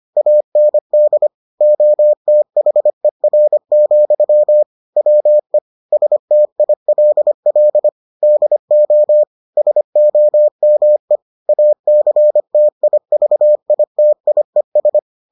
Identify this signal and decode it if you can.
Morse code, heard as a keyed tone, about 25 wpm, 600 Hz. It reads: AND OTHER, WE STILL DO SOME ACTIVITIES